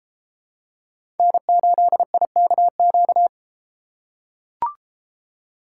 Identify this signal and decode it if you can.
Morse code, heard as a keyed tone, about 33 wpm, 700 Hz. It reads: N8IKQ